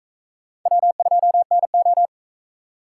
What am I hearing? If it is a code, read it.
Morse code, heard as a keyed tone, about 45 wpm, 700 Hz. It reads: W2NO